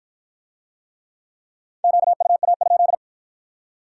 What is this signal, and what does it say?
Morse code, heard as a keyed tone, about 45 wpm, 700 Hz. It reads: QUA?